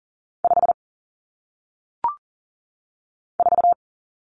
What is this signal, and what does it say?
Morse code, heard as a keyed tone, about 40 wpm, 700 Hz. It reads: 5 4
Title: Common Cut Numbers - 1-Digit 40wpm